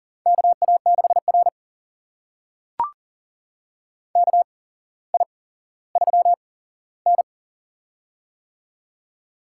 Morse code, heard as a keyed tone, about 44 wpm, spaced as for 17 wpm, 690 Hz. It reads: KA6R KI3N
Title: USA Call Signs - 4 Characters - 8x Character Spacing - 40wpm